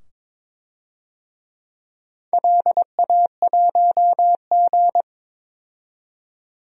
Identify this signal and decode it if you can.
Morse code, heard as a keyed tone, about 22 wpm, 700 Hz. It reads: LA1G
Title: DXCC-100 Call Signs - 4 Characters - 22wpm